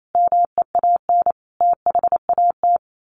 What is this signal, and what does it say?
Morse code, heard as a keyed tone, about 28 wpm, 700 Hz. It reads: MEAN THAT
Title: Sets of 5 Words Encore 28wpm